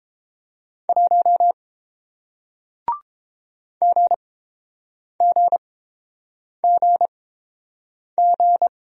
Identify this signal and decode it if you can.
Morse code, heard as a keyed tone, about 30 wpm, spaced as for 7 wpm, 700 Hz. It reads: 1 GGGG